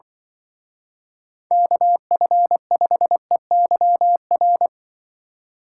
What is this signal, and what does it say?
Morse code, heard as a keyed tone, about 25 wpm, 700 Hz. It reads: KF5EYR
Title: Speed Racing - USA 2x3 Call Signs 20wpm